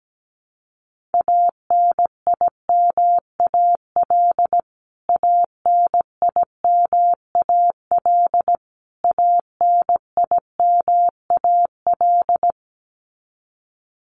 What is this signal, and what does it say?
Morse code, heard as a keyed tone, about 17 wpm, 700 Hz. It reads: ANIMAL ANIMAL ANIMAL
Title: Top 200 Words Repeated 17wpm